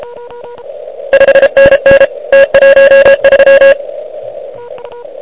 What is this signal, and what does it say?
Morse code, heard as a keyed tone, about 33 wpm, 1650 Hz, with noise. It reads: 5NN T13
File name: ARCHIV / QCX / QCX Pileup